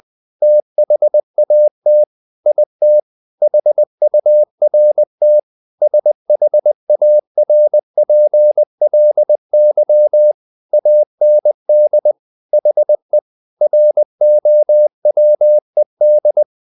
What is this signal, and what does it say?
Morse code, heard as a keyed tone, about 20 wpm, 600 Hz. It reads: THAT IT HURT SHARPLY AND HE ROWED
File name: the old man and the sea / the old man and the sea part 1 20wpm